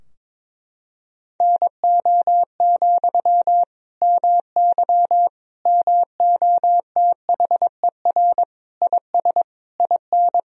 Morse code, heard as a keyed tone, about 22 wpm, 700 Hz. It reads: NO, MY MOTHER IS IN
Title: Sentences from Top 1000 Words - Review Part 2 - 22wpm